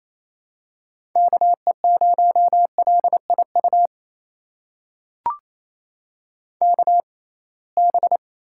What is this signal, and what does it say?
Morse code, heard as a keyed tone, about 28 wpm, spaced as for 21 wpm, 700 Hz. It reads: KE0LIU KB
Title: USA Call Signs - 6 Characters - 6x Character Spacing - 28wpm